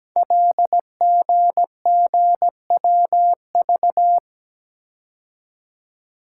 Morse code, heard as a keyed tone, about 17 wpm, 700 Hz. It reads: LGGWV
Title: Code Groups - 5 Letters - 17wpm